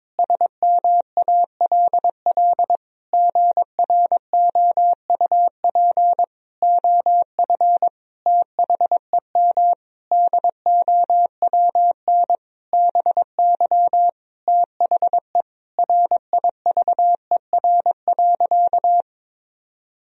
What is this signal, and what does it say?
Morse code, heard as a keyed tone, about 22 wpm, 700 Hz. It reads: SMALL GROUP OF THEM DOWN BY THE RIVER.
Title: Sentences from Top 1000 Words - Review Part 1 - 22wpm